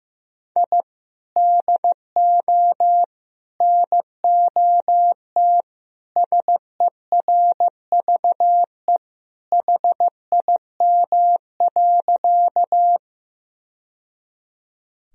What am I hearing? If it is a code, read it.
Morse code, heard as a keyed tone, about 16 wpm, 700 Hz. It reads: I DO NOT SERVE HIM.
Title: Sentences from Top 1000 Words - Review Part 2 - 15wpm